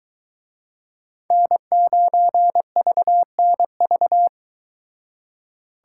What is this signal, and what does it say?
Morse code, heard as a keyed tone, about 22 wpm, 700 Hz. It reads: N9VNV